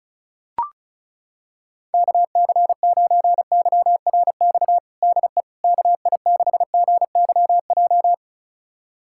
Morse code, heard as a keyed tone, about 36 wpm, 700 Hz. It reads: KC9YRX DE KI6GYJ
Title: USA Callsign DE Callsign 35wpm